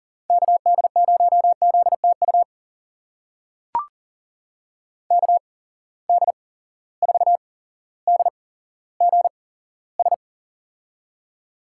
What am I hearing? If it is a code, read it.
Morse code, heard as a keyed tone, about 37 wpm, spaced as for 17 wpm, 700 Hz. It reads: KD0ZTU KD4DGS
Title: USA Call Signs - 6 Characters - 8x Character Spacing - 40wpm